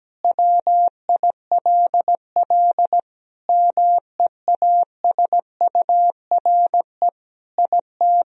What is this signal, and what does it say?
Morse code, heard as a keyed tone, about 17 wpm, 700 Hz. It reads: WILL MEASURE IT
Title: Sentences from Top 1000 Words - Review Part 2 - 17wpm